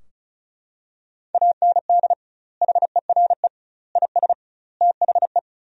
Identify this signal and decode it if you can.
Morse code, heard as a keyed tone, about 36 wpm, 700 Hz. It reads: AND HERE IS THE
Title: 2x Word-Spacing - Sentences from Top 500 Words - 35wpm